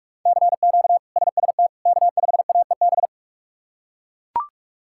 Morse code, heard as a keyed tone, about 41 wpm, 700 Hz. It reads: CQ SST K5AEB
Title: K1USN SST QSO Practice - 45wpm